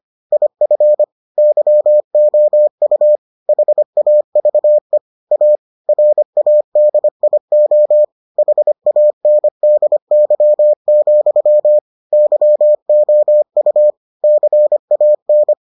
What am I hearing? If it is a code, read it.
Morse code, heard as a keyed tone, about 25 wpm, 600 Hz. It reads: IF YOU HAVE A RADIO HANDY, YOU CAN